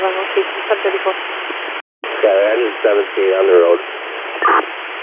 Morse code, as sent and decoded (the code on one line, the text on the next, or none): none